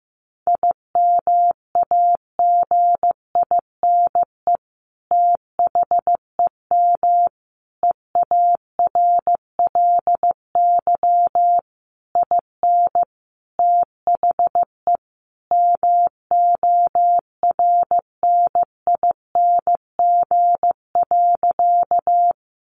Morse code, heard as a keyed tone, about 15 wpm, 700 Hz. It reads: IMAGINE THEM EARLY IN THE MORNING.
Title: Sentences from Top 1400 Words 15wpm